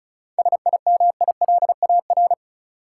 Morse code, heard as a keyed tone, about 36 wpm, 700 Hz. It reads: SIMILAR
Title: Sentences from Top 1000 Words - Review Part 5 - New Word Drill - 35wpm